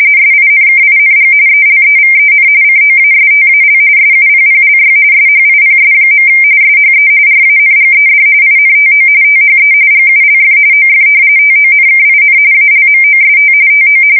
RTTY, about 45.45 baud. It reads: SINKING ELEVEN COASTAL SUPPLY SHIPS. SHE RANGED INTO THE GULF OF TONKIN ALMOST TO
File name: fsk170 / battleshipnj